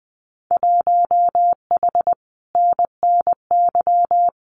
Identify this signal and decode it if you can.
Morse code, heard as a keyed tone, about 20 wpm, 700 Hz. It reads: 1H NNY
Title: Winter Field Day Exchanges 20wpm